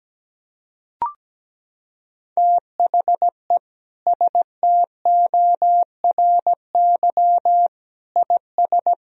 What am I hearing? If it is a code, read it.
Morse code, heard as a keyed tone, about 17 wpm, 700 Hz. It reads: THE STORY IS
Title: Sentences from Top 1000 Words - Review Part 4 - 17wpm